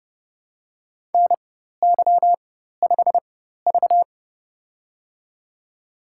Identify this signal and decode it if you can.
Morse code, heard as a keyed tone, about 30 wpm, spaced as for 14 wpm, 700 Hz. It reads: NY5V
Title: USA Call Signs - 4 Characters - 4x Character Spacing - 30wpm